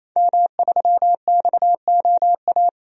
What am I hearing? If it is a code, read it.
Morse code, heard as a keyed tone, about 28 wpm, 700 Hz. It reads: M3XOA
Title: DXCC-100 Call Signs - 5 Characters - 28wpm